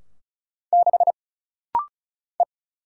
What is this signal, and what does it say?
Morse code, heard as a keyed tone, about 36 wpm, 700 Hz. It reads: 6 E
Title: Single Letter-Number Alt - Rapid-Fire 35wpm